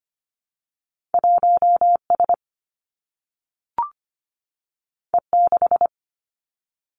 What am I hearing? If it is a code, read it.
Morse code, heard as a keyed tone, about 25 wpm, 700 Hz. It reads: 1S E6
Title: Two Challenging Letter-Number Combinations 25wpm